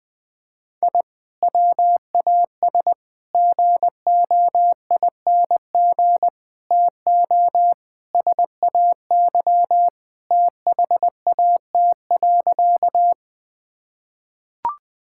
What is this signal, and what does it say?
Morse code, heard as a keyed tone, about 21 wpm, 700 Hz. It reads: I WAS GOING TO SAY THAT.
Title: Sentences with 5-Word Phrases - 20wpm